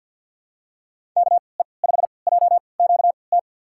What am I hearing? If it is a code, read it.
Morse code, heard as a keyed tone, about 47 wpm, spaced as for 28 wpm, 690 Hz. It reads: KE5JQT
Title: USA Call Signs - 6 Characters - 3x Character Spacing - 50wpm